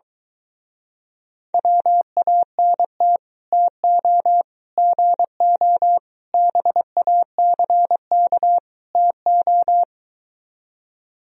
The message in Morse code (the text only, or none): WANT TO GO BACK TO